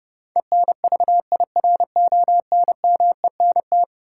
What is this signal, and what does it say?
Morse code, heard as a keyed tone, about 31 wpm, 700 Hz. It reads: ENVIRONMENT
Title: Sentences from Top 1000 Words - Review Part 5 - New Word Drill - 30wpm